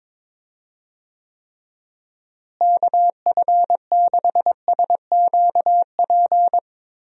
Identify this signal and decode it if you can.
Morse code, heard as a keyed tone, about 21 wpm, 700 Hz. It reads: KF6SQP
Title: USA Call Signs - 6 Characters - Repeated 2x - 22wpm